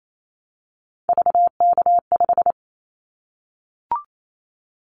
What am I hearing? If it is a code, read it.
Morse code, heard as a keyed tone, about 28 wpm, 700 Hz. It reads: VK5
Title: Callsign Prefixes for Top 100 DXCC - 28wpm